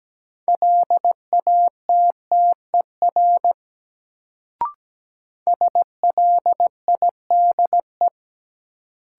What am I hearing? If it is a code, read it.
Morse code, heard as a keyed tone, about 17 wpm, 700 Hz. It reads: LATTER SLIDE
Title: Sentences from Top 1600 Words - New Word Drill - 17wpm